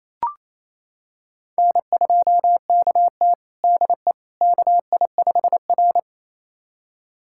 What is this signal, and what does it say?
Morse code, heard as a keyed tone, about 28 wpm, 700 Hz. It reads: N2KT DE KI5R
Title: USA Callsign DE Callsign - 4 Characters - 28wpm